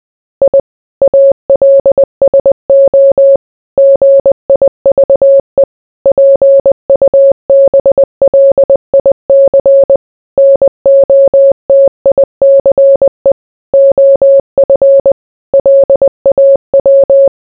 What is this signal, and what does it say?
Morse code, heard as a keyed tone, about 20 wpm, 550 Hz. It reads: I ALSO GIVE PUBLIC NOTICE OF LAW